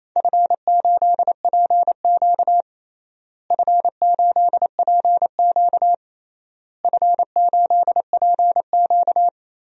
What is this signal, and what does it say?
Morse code, heard as a keyed tone, about 27 wpm, 700 Hz. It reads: F8PQ F8PQ F8PQ
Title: DXCC-100 Call Signs - 4 Characters - Repeated 3x - 28wpm